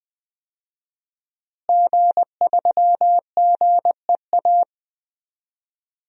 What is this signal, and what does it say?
Morse code, heard as a keyed tone, about 20 wpm, 700 Hz. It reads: G3GEA
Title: DXCC-100 Call Signs - 5 Characters - Repeated 3x - 20wpm